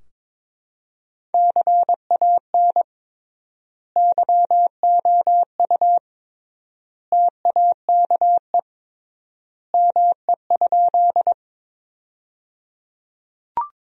Morse code, heard as a keyed tone, about 22 wpm, 700 Hz. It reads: CAN YOU TAKE ME?
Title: 3x Word-Spacing - Sentences from Top 200 Words - 22wpm